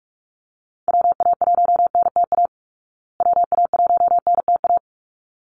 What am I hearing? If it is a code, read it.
Morse code, heard as a keyed tone, about 46 wpm, 700 Hz. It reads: WA1NTA WA1NTA
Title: Call Signs Repeated 3x - 45wpm